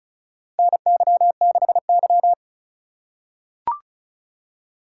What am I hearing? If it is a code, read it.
Morse code, heard as a keyed tone, about 36 wpm, 700 Hz. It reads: NY6Y
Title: K1USN SST QSO Practice - 35wpm